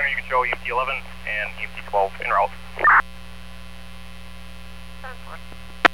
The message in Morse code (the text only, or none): none